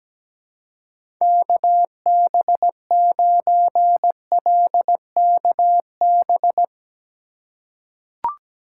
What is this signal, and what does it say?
Morse code, heard as a keyed tone, about 17 wpm, 700 Hz. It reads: KB9LKB
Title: USA Call Signs - 6 Characters - 3x Character Spacing - 17wpm